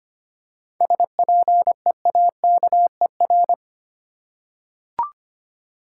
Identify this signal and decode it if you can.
Morse code, heard as a keyed tone, about 25 wpm, 700 Hz. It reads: SPEAKER
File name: practice / 25wpm / Easy Words with Hints - 25wpm